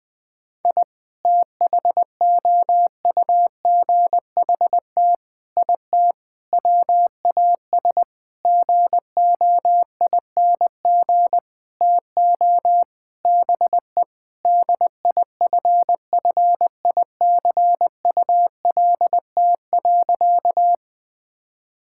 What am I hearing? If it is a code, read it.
Morse code, heard as a keyed tone, about 20 wpm, 700 Hz. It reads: I THOUGHT IT WAS GOING TO BE DIFFICULT.
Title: Sentences with 7-Word Phrases - Spoken Beginning - 20wpm